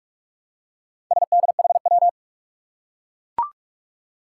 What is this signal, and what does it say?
Morse code, heard as a keyed tone, about 44 wpm, 700 Hz. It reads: SDHW
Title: Four Challenging Letter-Number Combinations 45wpm